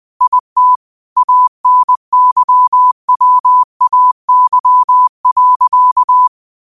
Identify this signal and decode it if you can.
Morse code, heard as a keyed tone, about 20 wpm, 1000 Hz. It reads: IT ANYWAY.